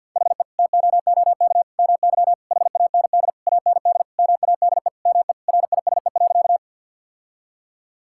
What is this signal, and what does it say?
Morse code, heard as a keyed tone, about 50 wpm, 700 Hz. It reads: HE TOOK MY HAND AND MADE ME RISE.
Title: Sentences from Top 1000 Words - Review Part 6 - 50wpm